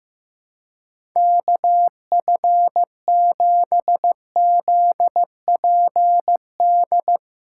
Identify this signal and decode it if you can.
Morse code, heard as a keyed tone, about 15 wpm, 700 Hz. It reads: KF7ZPD
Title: USA Call Signs - 4x Character Spacing - 15wpm